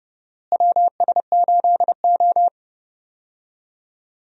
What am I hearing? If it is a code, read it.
Morse code, heard as a keyed tone, about 31 wpm, 700 Hz. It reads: WS8O
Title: USA Call Signs - 4 Characters - Repeated 2x - 30wpm